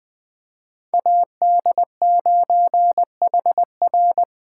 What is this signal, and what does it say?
Morse code, heard as a keyed tone, about 21 wpm, 700 Hz. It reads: AD9HR
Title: Speed Racing - USA 2x2 Call Signs 17wpm